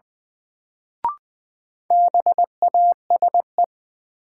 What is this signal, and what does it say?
Morse code, heard as a keyed tone, about 20 wpm, 700 Hz. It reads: BASE
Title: Sentences from Top 1000 Words - Review Part 1 - New Word Drill - 20wpm